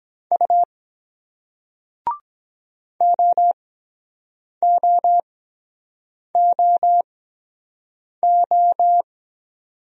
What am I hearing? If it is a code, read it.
Morse code, heard as a keyed tone, about 23 wpm, spaced as for 7 wpm, 700 Hz. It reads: U OOOO